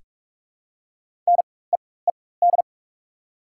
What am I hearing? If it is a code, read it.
Morse code, heard as a keyed tone, about 44 wpm, spaced as for 21 wpm, 690 Hz. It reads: NEED
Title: 4x Character Spacing - Top 300 Words - 45wpm